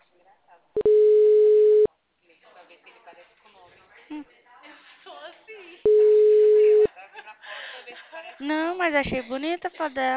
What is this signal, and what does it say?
Ringback tone (Europe, 425 Hz), with speech.
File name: out- -2004- - - .614